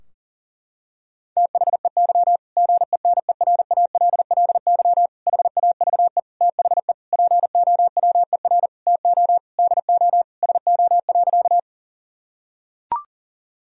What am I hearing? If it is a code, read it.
Morse code, heard as a keyed tone, about 40 wpm, 700 Hz. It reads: THEY GENERALLY HAVE THE POWER TO DO SO.
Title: Sentences from Top 1000 Words - Review Part 4 - 40wpm